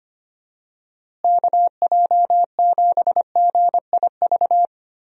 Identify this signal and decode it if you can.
Morse code, heard as a keyed tone, about 25 wpm, 700 Hz. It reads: KJ7GIV
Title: USA Call Signs - 6 Characters - Repeated 2x - 25wpm